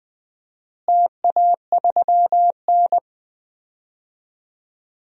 Morse code, heard as a keyed tone, about 20 wpm, 700 Hz. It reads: TA3N
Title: DXCC-100 Call Signs - 4 Characters - Repeated 2x - 20wpm